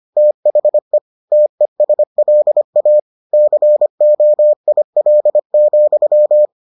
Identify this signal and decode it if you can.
Morse code, heard as a keyed tone, about 25 wpm, 600 Hz. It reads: THE TESLA COIL,